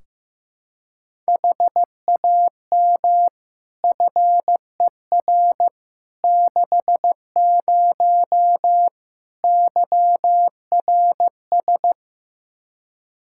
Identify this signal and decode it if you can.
Morse code, heard as a keyed tone, about 16 wpm, 700 Hz. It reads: HAM FER 60 YRS
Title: Practice QSO - Exchange 3 - More Info - 15wpm